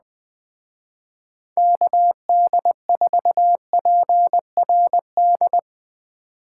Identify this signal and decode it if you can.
Morse code, heard as a keyed tone, about 20 wpm, 700 Hz. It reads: KD4PRD
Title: Speed Racing - USA 2x3 Call Signs 17wpm